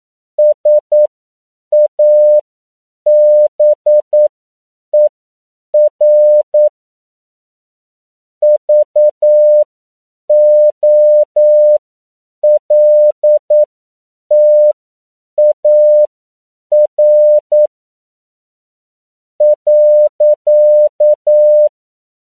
Morse code, heard as a keyed tone, about 9 wpm, spaced as for 7 wpm, 600 Hz. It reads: SABER VOLTAR .
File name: CW / 10PPM